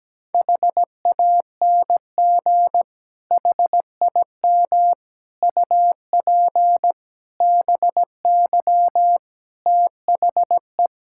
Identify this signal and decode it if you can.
Morse code, heard as a keyed tone, about 17 wpm, 700 Hz. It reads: HANG HIM UP BY THE
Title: Sentences from Top 1600 Words 17wpm